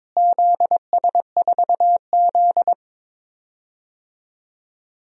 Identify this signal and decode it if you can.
Morse code, heard as a keyed tone, about 22 wpm, 700 Hz. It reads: ZS4Z
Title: DXCC-100 Call Signs - 4 Characters - Repeated 2x - 22wpm